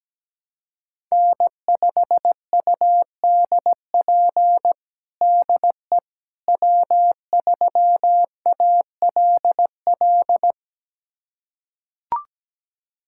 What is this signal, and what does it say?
Morse code, heard as a keyed tone, about 17 wpm, 700 Hz. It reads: N5UDP DE W3ALL
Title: USA Callsign DE Callsign - 5 Characters - 17wpm